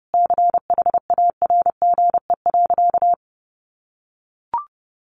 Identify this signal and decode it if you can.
Morse code, heard as a keyed tone, about 30 wpm, 700 Hz. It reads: CHARGE.
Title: Sentences from Top 1300 Words 30wpm